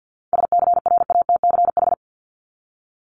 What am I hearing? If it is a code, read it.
Morse code, heard as a keyed tone, about 50 wpm, 700 Hz. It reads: SCRATCH
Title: Top 1501-1750 Words 50wpm